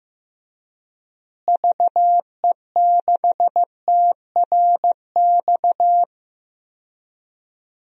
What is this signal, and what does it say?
Morse code, heard as a keyed tone, about 15 wpm, 700 Hz. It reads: VE6TRX
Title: DXCC-100 Call Signs - 6 Characters - Repeated 2x - 15wpm